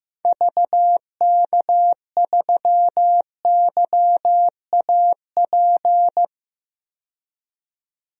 Morse code, heard as a keyed tone, about 15 wpm, 690 Hz. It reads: VK3YAP
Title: DXCC-100 Call Signs - 6 Characters - 15wpm